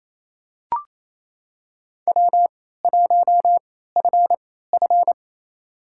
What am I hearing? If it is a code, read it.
Morse code, heard as a keyed tone, about 28 wpm, spaced as for 16 wpm, 700 Hz. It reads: W1FF
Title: USA Call Signs - 4 Characters - 3x Character Spacing - 28wpm